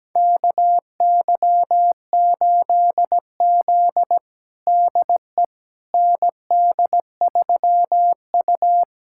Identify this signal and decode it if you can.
Morse code, heard as a keyed tone, about 17 wpm, 700 Hz. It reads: KY8Z DE ND3U
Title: USA Callsign DE Callsign - 4 Characters - 17wpm